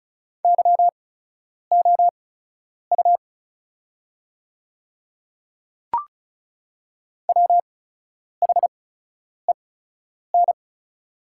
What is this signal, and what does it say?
Morse code, heard as a keyed tone, about 36 wpm, spaced as for 10 wpm, 690 Hz. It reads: YOU WHEN
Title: 8x Character Spacing - Top 200 Words - 35wpm